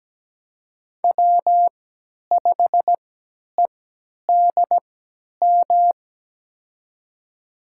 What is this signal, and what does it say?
Morse code, heard as a keyed tone, about 17 wpm, spaced as for 10 wpm, 700 Hz. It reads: W5EDM
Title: USA Call Signs - 5 Characters - 3x Character Spacing - 17wpm